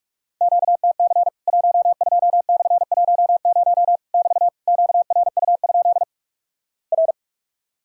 Morse code, heard as a keyed tone, about 46 wpm, 700 Hz. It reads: QTC 12/10 <BT> QRU? R
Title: Worked All Europe DX Contest - QTC Exchange - 45wpm